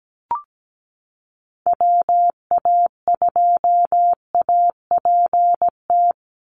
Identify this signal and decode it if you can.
Morse code, heard as a keyed tone, about 17 wpm, 700 Hz. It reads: WA2APT
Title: USA Call Signs 17wpm